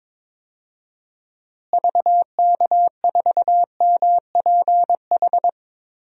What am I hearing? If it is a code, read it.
Morse code, heard as a keyed tone, about 22 wpm, 700 Hz. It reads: VK4MPH
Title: DXCC-100 Call Signs - 6 Characters - Repeated 3x - 22wpm